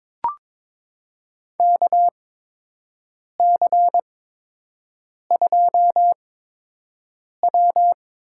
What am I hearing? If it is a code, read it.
Morse code, heard as a keyed tone, about 22 wpm, spaced as for 6 wpm, 700 Hz. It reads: KC2W
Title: USA Call Signs - 8x Character Spacing - 22wpm